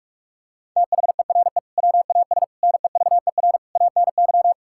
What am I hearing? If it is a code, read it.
Morse code, heard as a keyed tone, about 46 wpm, 700 Hz. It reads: THERE WAS NEVER ANY
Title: Sentences from Top 1000 Words - Review Part 6 - 45wpm